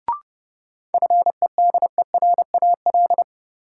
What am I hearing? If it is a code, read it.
Morse code, heard as a keyed tone, about 32 wpm, 700 Hz. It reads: FEDERAL
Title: Sets of 3 Words - New Word Drill - 30wpm